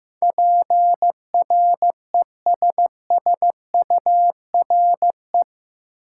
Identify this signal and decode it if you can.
Morse code, heard as a keyed tone, about 15 wpm, 700 Hz. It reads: PRESSURE
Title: Sentences from Top 1000 Words - Review Part 6 - New Word Drill - 15wpm